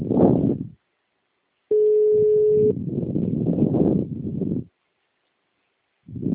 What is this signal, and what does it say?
Ringback tone (Europe, 425 Hz).
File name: out- -2009- - - .313